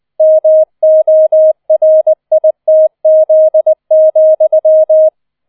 Morse code, heard as a keyed tone, about 20 wpm, 630 Hz. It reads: MORITZ,